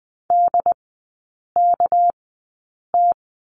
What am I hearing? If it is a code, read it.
Morse code, heard as a keyed tone, about 20 wpm, spaced as for 8 wpm, 690 Hz. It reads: DKT
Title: 205 Lesson 33e - Letter j - Three Characters 20wpm